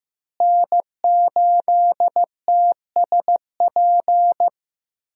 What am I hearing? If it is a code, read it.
Morse code, heard as a keyed tone, about 15 wpm, 700 Hz. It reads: N8TSP
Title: USA Call Signs - 5 Characters - Repeated 2x - 15wpm